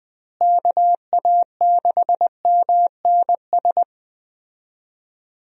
Morse code, heard as a keyed tone, about 21 wpm, 700 Hz. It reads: KA6MNS
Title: USA Call Signs - 6 Characters - Repeated 3x - 20wpm